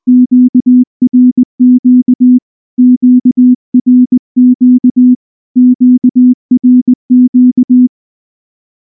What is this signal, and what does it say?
Morse code, heard as a keyed tone, about 20 wpm, 250 Hz. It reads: QRQ QRQ QRQ
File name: morse / morse-training-20WPM-2